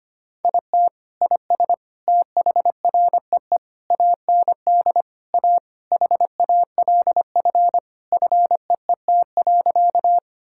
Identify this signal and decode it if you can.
Morse code, heard as a keyed tone, about 25 wpm, 700 Hz. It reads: IT IS THREE AND A HALF FEET.
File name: practice / 25wpm / Sentences with 3-Word Phrases - 25wpm